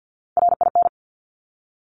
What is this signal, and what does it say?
Morse code, heard as a keyed tone, about 50 wpm, 700 Hz. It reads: RIN
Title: Top Three Letter Combinations in Words 50wpm